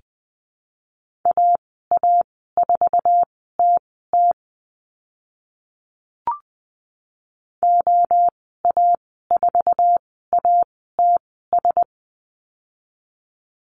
Morse code, heard as a keyed tone, about 20 wpm, spaced as for 14 wpm, 700 Hz. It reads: AA4TT OA4ATS
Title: 062 Lesson 11g - Letter h - Call Signs 20wpm